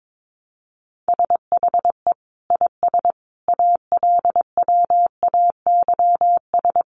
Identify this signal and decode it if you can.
Morse code, heard as a keyed tone, about 22 wpm, 700 Hz. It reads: SHE IS ALWAYS
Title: Sentences from Top 1200 Words 22wpm